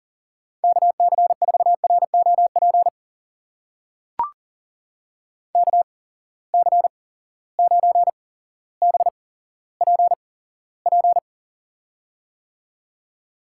Morse code, heard as a keyed tone, about 40 wpm, spaced as for 17 wpm, 700 Hz. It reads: KC4ROP KC9BPP
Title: USA Call Signs - 6 Characters - 8x Character Spacing - 40wpm